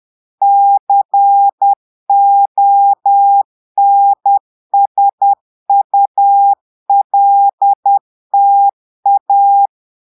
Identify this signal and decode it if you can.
Morse code, heard as a keyed tone, about 10 wpm, 800 Hz. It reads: CONSULTA
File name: Portugais - texte en clair 1 - vitesse - 10 mots minute